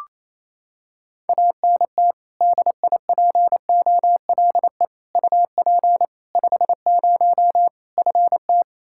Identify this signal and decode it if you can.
Morse code, heard as a keyed tone, about 27 wpm, 700 Hz. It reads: ANT DIPOLE UP 50 FT
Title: Practice QSO - Exchange 2 - Info - 28wpm